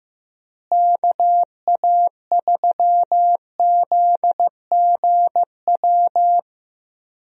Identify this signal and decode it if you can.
Morse code, heard as a keyed tone, about 15 wpm, 700 Hz. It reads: KA3ZGW
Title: USA Call Signs - 6 Characters - Repeated 3x - 15wpm